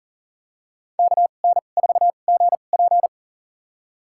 Morse code, heard as a keyed tone, about 40 wpm, spaced as for 29 wpm, 700 Hz. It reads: KN4GP
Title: USA Call Signs - 5 Characters - 2x Character Spacing - 40wpm